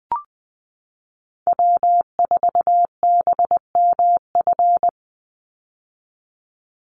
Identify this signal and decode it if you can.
Morse code, heard as a keyed tone, about 20 wpm, 700 Hz. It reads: W4BMF
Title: USA Call Signs 20wpm